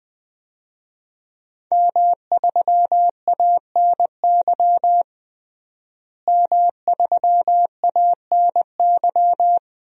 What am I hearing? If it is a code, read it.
Morse code, heard as a keyed tone, about 21 wpm, 700 Hz. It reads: M3ANY M3ANY
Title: DXCC-100 Call Signs - 5 Characters - Repeated 3x - 20wpm